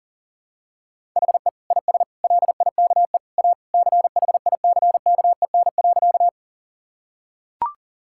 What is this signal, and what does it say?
Morse code, heard as a keyed tone, about 37 wpm, 700 Hz. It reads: HE IS LIKE A CHICKEN.
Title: Sentences from Top 1000 Words - Review Part 5 - 40wpm